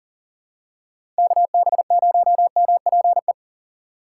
Morse code, heard as a keyed tone, about 38 wpm, 700 Hz. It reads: KB0MPE
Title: USA Call Signs - 3x Character Spacing - 40wpm